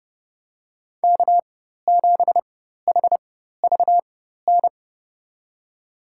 Morse code, heard as a keyed tone, about 30 wpm, spaced as for 14 wpm, 700 Hz. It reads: K7HVN
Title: USA Call Signs - 5 Characters - 4x Character Spacing - 30wpm